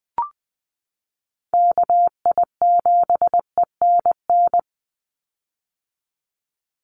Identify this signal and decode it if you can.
Morse code, heard as a keyed tone, about 20 wpm, 700 Hz. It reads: KI7ENN
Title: USA Call Signs 20wpm